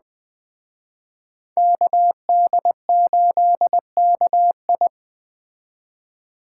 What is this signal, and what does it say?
Morse code, heard as a keyed tone, about 20 wpm, 700 Hz. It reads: KD8KI